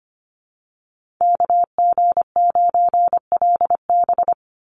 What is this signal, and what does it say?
Morse code, heard as a keyed tone, about 25 wpm, 700 Hz. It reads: KG9LB
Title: Call Signs - 25wpm